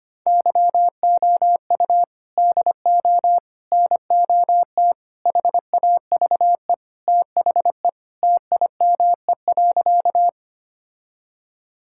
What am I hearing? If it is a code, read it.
Morse code, heard as a keyed tone, about 25 wpm, 700 Hz. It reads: YOU DO NOT HAVE THE TIME.
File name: practice / 25wpm / Sentences with 4-Word Phrases - 25wpm